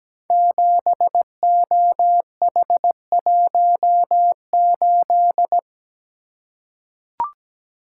Morse code, heard as a keyed tone, about 17 wpm, 700 Hz. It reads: 7OH18
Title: Code Groups - 5 Letter:Numbers - 17wpm